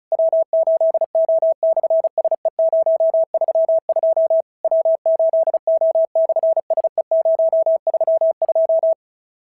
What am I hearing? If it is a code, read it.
Morse code, heard as a keyed tone, about 36 wpm, 650 Hz. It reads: W8O/SE032 W8O/SE032
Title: SOTA QSO Practice 35wpm